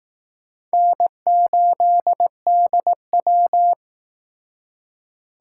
Morse code, heard as a keyed tone, about 19 wpm, 700 Hz. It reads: N8DW